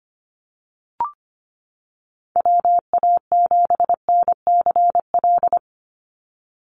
Morse code, heard as a keyed tone, about 25 wpm, 700 Hz. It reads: WA7NCL
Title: NA SOTA Activator Callsigns 25wpm